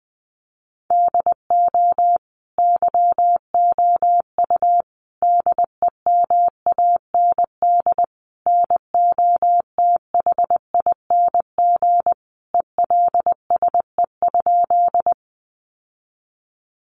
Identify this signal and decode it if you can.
Morse code, heard as a keyed tone, about 20 wpm, 700 Hz. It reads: DO YOU DEMAND NOTHING ELSE?
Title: Sentences from Top 800 Words 20wpm